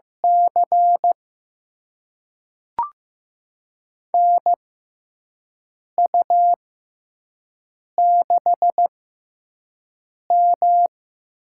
Morse code, heard as a keyed tone, about 15 wpm, spaced as for 5 wpm, 700 Hz. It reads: C NU6M
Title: USA Call Signs - 4 Characters - 6x Character Spacing - 15wpm